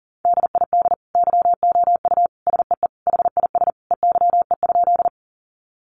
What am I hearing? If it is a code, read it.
Morse code, heard as a keyed tone, about 40 wpm, 700 Hz. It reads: DID YOU SEE HIS EYE?
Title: Sentences from Top 300 Words 40wpm